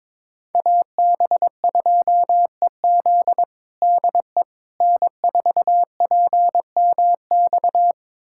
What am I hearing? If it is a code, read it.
Morse code, heard as a keyed tone, about 22 wpm, 700 Hz. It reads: AB2EZ DE N4PMX
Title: USA Callsign DE Callsign - 5 Characters - 22wpm